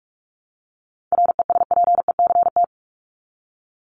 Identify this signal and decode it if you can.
Morse code, heard as a keyed tone, about 44 wpm, 700 Hz. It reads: RESPECT
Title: Top 1501-1750 Words 45wpm